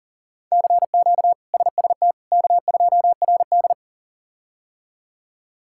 Morse code, heard as a keyed tone, about 40 wpm, 700 Hz. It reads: CQ SST K2RD
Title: K1USN SST QSO Practice - 40wpm